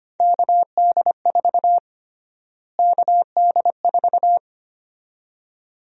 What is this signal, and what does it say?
Morse code, heard as a keyed tone, about 25 wpm, 700 Hz. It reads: KD4 KD4
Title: USA Call Sign Prefixes - Repeated 2x - 25wpm